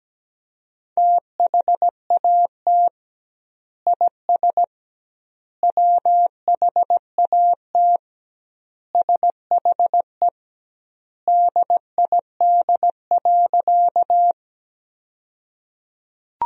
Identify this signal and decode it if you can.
Morse code, heard as a keyed tone, about 17 wpm, 700 Hz. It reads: THAT IS WHAT SHE DID.
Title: 2x Word-Spacing - Sentences from Top 100 Words - 17wpm